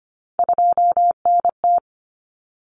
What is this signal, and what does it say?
Morse code, heard as a keyed tone, about 25 wpm, 700 Hz. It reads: 2NT
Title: Common Cut Numbers - 3-Digit 25wpm